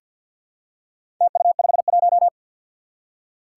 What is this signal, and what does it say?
Morse code, heard as a keyed tone, about 51 wpm, 700 Hz. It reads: TU51
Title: Four Challenging Letter-Number Combinations 50wpm